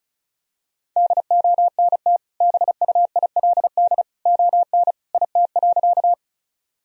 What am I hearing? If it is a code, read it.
Morse code, heard as a keyed tone, about 36 wpm, 700 Hz. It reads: DONT BUILD ON IT.
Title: Sentences from Top 1000 Words - Review Part 1 - 35wpm